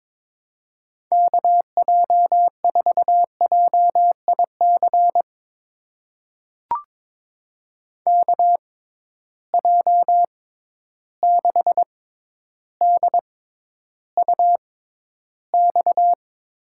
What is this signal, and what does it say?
Morse code, heard as a keyed tone, about 22 wpm, spaced as for 11 wpm, 700 Hz. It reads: KJ4JIC KJ6DUX
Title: USA Call Signs - 6 Characters - 6x Character Spacing - 22wpm